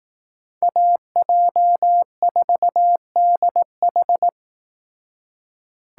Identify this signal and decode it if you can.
Morse code, heard as a keyed tone, about 18 wpm, 700 Hz. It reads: AJ4DH
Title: Speed Racing - USA 2x2 Call Signs 15wpm